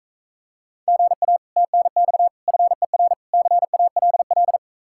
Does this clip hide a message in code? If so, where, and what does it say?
Yes: Morse code from 0.9 to 4.6 s, heard as a keyed tone, about 44 wpm, 700 Hz. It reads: GA TNX FER CALL